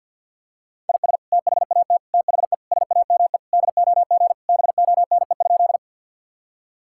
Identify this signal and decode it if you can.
Morse code, heard as a keyed tone, about 50 wpm, 700 Hz. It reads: IS THAT THE SAME DOG BONE?
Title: Sentences from Top 1600 Words 50wpm